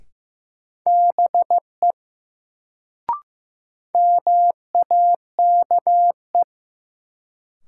Morse code, heard as a keyed tone, about 15 wpm, 700 Hz. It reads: BE MAKE
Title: Sets of 2 Words - New Word Drill - 15wpm